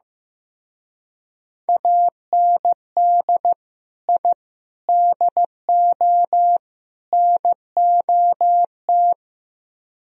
Morse code, heard as a keyed tone, about 17 wpm, 700 Hz. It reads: AND I DO NOT
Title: Speed-Racing - Sets of 4 Words 15wpm